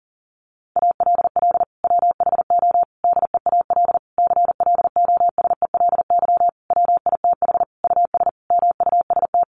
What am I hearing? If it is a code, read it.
Morse code, heard as a keyed tone, about 40 wpm, 700 Hz. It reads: ALL WHO DEAL CLOSELY WITH US MUST
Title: Sentences from Top 1400 Words 40wpm